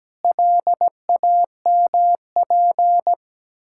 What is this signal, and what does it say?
Morse code, heard as a keyed tone, about 17 wpm, 700 Hz. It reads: LAMP
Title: Pangram Challenge v2 - 17wpm